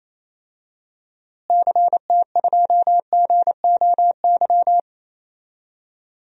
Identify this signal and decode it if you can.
Morse code, heard as a keyed tone, about 28 wpm, 700 Hz. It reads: CT2GOY
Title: DXCC-100 Call Signs - 6 Characters - Repeated 2x - 28wpm